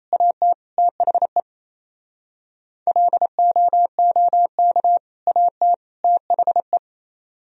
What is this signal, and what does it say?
Morse code, heard as a keyed tone, about 35 wpm, 700 Hz. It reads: AT THE LOOK AT THE